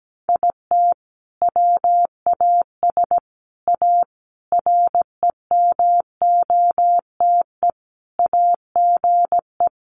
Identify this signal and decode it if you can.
Morse code, heard as a keyed tone, about 17 wpm, 700 Hz. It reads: IT WAS A REMOTE AGE
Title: Sentences from Top 1300 Words 17wpm